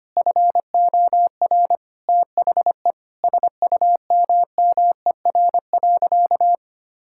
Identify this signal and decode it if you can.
Morse code, heard as a keyed tone, about 25 wpm, 700 Hz. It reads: FOR THE SUMMER.
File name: practice / 25wpm / Sentences with 9-Word Phrases - 25wpm